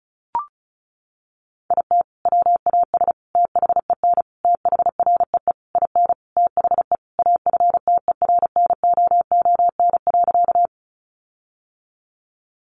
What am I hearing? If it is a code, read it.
Morse code, heard as a keyed tone, about 34 wpm, 700 Hz. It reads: IT WAS THEN THREE IN THE AFTERNOON.
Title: Sentences from Top 1500 Words 35wpm